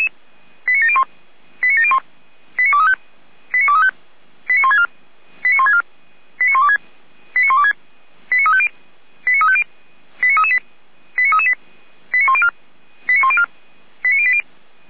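Five-tone selective call (ZVEI), heard as quick runs of five tones, tones of about 70 ms. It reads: 89701 89701 89235 89235 89164 89164 89126 89126 89350 89350 89338 89338 89104 89104 89080
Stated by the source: ZVEI five-tone selective call 89590 received by SDR.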